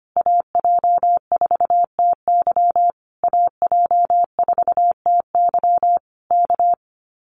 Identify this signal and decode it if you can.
Morse code, heard as a keyed tone, about 25 wpm, 700 Hz. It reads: AJ4TY AJ4TY K
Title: Common QSO Phrases Encore 25wpm